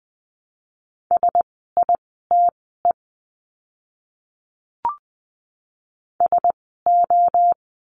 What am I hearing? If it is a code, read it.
Morse code, heard as a keyed tone, about 20 wpm, spaced as for 14 wpm, 700 Hz. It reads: SITE SO
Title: 101 Lesson 17f - Letter u - Words 20wpm